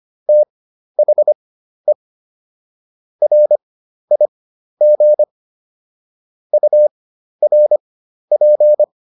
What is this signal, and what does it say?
Morse code, heard as a keyed tone, about 25 wpm, spaced as for 12 wpm, 600 Hz. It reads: THE RIG URP